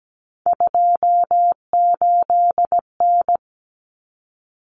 Morse code, heard as a keyed tone, about 17 wpm, 700 Hz. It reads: 28N
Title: Common Cut Numbers - 3-Digit 17wpm